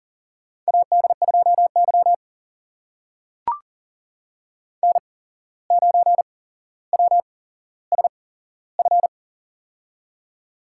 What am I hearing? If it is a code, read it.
Morse code, heard as a keyed tone, about 40 wpm, spaced as for 16 wpm, 700 Hz. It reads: AD2Y N9WSF
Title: USA Call Signs - 8x Character Spacing - 40wpm